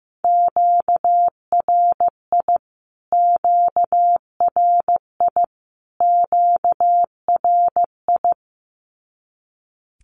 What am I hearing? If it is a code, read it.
Morse code, heard as a keyed tone, about 15 wpm, 700 Hz. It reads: QRI QRI QRI
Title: Q-Codes Repeated 3x 15wpm